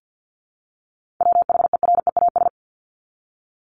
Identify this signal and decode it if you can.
Morse code, heard as a keyed tone, about 50 wpm, 700 Hz. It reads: WHEREAS
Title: Top 1501-1750 Words 50wpm